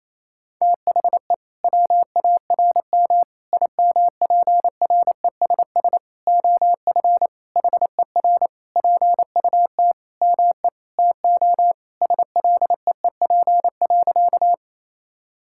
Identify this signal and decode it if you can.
Morse code, heard as a keyed tone, about 28 wpm, 700 Hz. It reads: THE WARM IMPRESS OF HER PUT ME TO SLEEP.
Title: Sentences from Top 1600 Words 28wpm